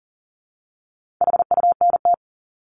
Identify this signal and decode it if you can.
Morse code, heard as a keyed tone, about 40 wpm, 700 Hz. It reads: HUNT
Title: Top 1501-1750 Words 40wpm